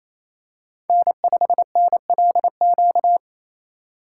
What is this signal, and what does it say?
Morse code, heard as a keyed tone, about 29 wpm, 700 Hz. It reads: N5NLQ
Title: USA Call Signs - 5 Characters - 2x Character Spacing - 28wpm